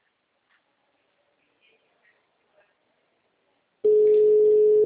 Ringback tone (Europe, 425 Hz).